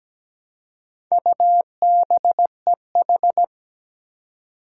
Morse code, heard as a keyed tone, about 17 wpm, 700 Hz. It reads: UBEH
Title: Four Challenging Letter-Number Combinations 17wpm